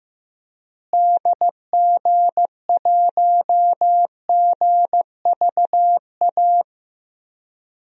Morse code, heard as a keyed tone, about 15 wpm, 700 Hz. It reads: DG1GVA
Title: DXCC-100 Call Signs - 6 Characters - Repeated 2x - 15wpm